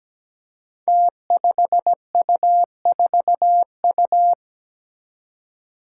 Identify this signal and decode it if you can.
Morse code, heard as a keyed tone, about 17 wpm, 700 Hz. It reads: T5U4U
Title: Five Challenging Letter-Number Combinations 17wpm